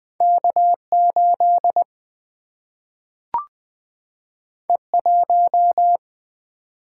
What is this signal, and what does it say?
Morse code, heard as a keyed tone, about 20 wpm, 700 Hz. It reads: K 8 E 1
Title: Single-Letter Single-Number 20wpm